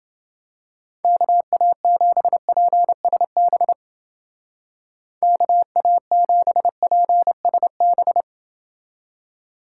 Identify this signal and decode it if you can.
Morse code, heard as a keyed tone, about 28 wpm, 700 Hz. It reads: KA7PSB KA7PSB